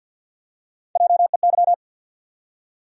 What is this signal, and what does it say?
Morse code, heard as a keyed tone, about 47 wpm, 700 Hz. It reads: JEY
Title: USA Call Sign Postfixes - 3 Letters 50wpm